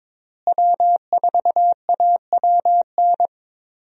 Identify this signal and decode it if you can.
Morse code, heard as a keyed tone, about 22 wpm, 700 Hz. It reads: W4AWN
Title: USA Call Signs - 5 Characters - Repeated 3x - 22wpm